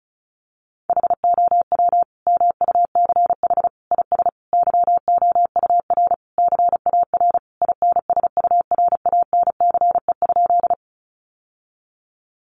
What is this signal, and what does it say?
Morse code, heard as a keyed tone, about 34 wpm, 700 Hz. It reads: HOW MUCH IS YOUR CAR INSURANCE?
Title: Sentences from Top 1100 Words 35wpm